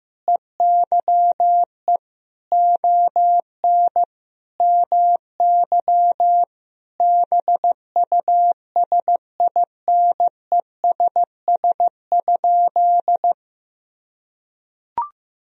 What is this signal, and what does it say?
Morse code, heard as a keyed tone, about 15 wpm, 700 Hz. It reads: EYE ON MY BUSINESS?
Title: Sentences from Top 1000 Words - Review Part 1 - 15wpm